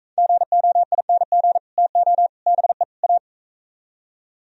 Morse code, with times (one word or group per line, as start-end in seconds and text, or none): 0.2-1.6 GOING
1.8-2.3 TO
2.5-2.8 BE
3.0-3.2 A